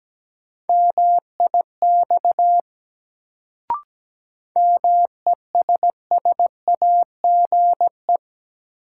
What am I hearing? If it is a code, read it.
Morse code, heard as a keyed tone, about 17 wpm, 700 Hz. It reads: MIX MESSAGE
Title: Sentences from Top 1100 Words - New Word Drill Alt - 17wpm